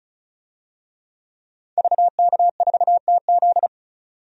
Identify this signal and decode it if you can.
Morse code, heard as a keyed tone, about 36 wpm, 700 Hz. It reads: VK4TZ
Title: DXCC-100 Call Signs - 5 Characters - 35wpm